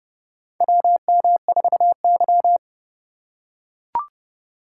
Morse code, heard as a keyed tone, about 30 wpm, 700 Hz. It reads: WM4Y
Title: USA Call Signs - 4 Characters - 2x Character Spacing - 30wpm